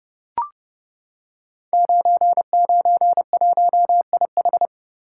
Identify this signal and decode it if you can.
Morse code, heard as a keyed tone, about 30 wpm, 700 Hz. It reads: 991IH
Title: Code Groups - 5 Letter:Numbers - 30wpm